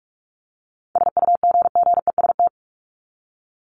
Morse code, heard as a keyed tone, about 44 wpm, 700 Hz. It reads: SUGGEST
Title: Top 1251-1500 Words 45wpm